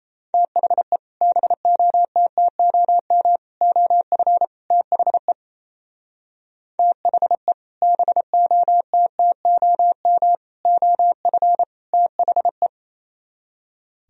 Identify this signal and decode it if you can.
Morse code, heard as a keyed tone, about 35 wpm, 700 Hz. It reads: THE BOTTOM OF THE THE BOTTOM OF THE